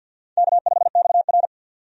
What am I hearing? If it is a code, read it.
Morse code, heard as a keyed tone, about 49 wpm, 700 Hz. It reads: K5XR
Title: USA Call Signs - 4 Characters - 50wpm